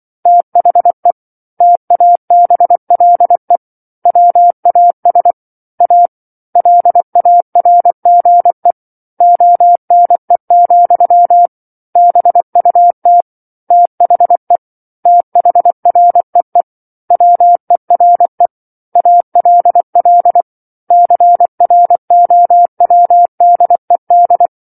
Morse code, heard as a keyed tone, about 24 wpm, 700 Hz. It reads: THE TABLE WAS A LARGE ONE, BUT THE THREE WERE ALL CROWDED
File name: 07-Chapter-07 24 WPM